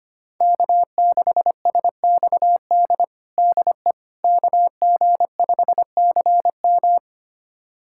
Morse code, heard as a keyed tone, about 25 wpm, 700 Hz. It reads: K6SXD DE KG5CM
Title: USA Callsign DE Callsign - 5 Characters - 25wpm